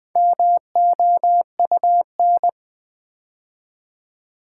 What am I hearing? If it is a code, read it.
Morse code, heard as a keyed tone, about 20 wpm, 700 Hz. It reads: MOUN
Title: Build A Word - 4 Letters At A Time - 8-Letter Words - 20wpm